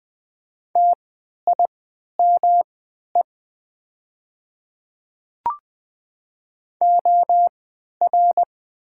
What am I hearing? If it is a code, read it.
Morse code, heard as a keyed tone, about 20 wpm, spaced as for 11 wpm, 700 Hz. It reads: TIME OR
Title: 3x Character Spacing - Top 100 Words - 20wpm